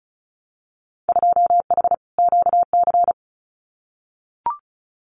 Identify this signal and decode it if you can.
Morse code, heard as a keyed tone, about 36 wpm, 700 Hz. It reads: 2H QC
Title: Winter Field Day Exchanges Encore 35wpm